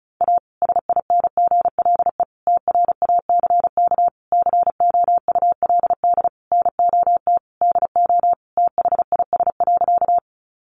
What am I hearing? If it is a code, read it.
Morse code, heard as a keyed tone, about 34 wpm, 700 Hz. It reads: A SINGLE TRACK COULD NOT DO THIS.
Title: Sentences from Top 1100 Words 35wpm